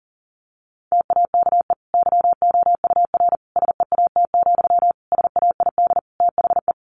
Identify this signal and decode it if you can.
Morse code, heard as a keyed tone, about 40 wpm, 700 Hz. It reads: TAKE YOUR SEAT, SAID THE
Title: Sentences from Top 1500 Words 40wpm